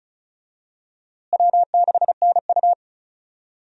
Morse code, heard as a keyed tone, about 36 wpm, 700 Hz. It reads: W6NU
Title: USA Call Signs - 4 Characters - Repeated 3x - 35wpm